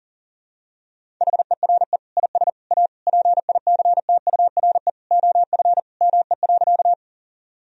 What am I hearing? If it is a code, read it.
Morse code, heard as a keyed tone, about 37 wpm, 700 Hz. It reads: HERE IS A PICTURE OF ME.
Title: Sentences from Top 1000 Words - Review Part 1 - 40wpm